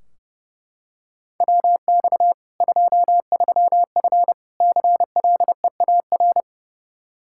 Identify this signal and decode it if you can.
Morse code, heard as a keyed tone, about 30 wpm, 700 Hz. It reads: WX 23F CLEAR
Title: K1USN SST QSO Practice 30wpm